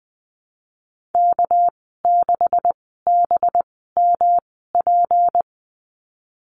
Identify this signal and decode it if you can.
Morse code, heard as a keyed tone, about 20 wpm, spaced as for 14 wpm, 700 Hz. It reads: K6BMP
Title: 245 Lesson 39g - Letter z - Call Signs 20wpm